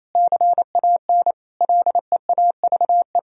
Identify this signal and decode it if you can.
Morse code, heard as a keyed tone, about 27 wpm, 700 Hz. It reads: CAN LEAVE
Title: Sentences from Top 1000 Words - Review Part 3 - 28wpm